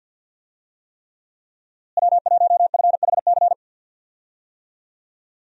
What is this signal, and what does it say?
Morse code, heard as a keyed tone, about 52 wpm, 700 Hz. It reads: W1FHC
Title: USA Call Signs - 5 Characters - Repeated 3x - 50wpm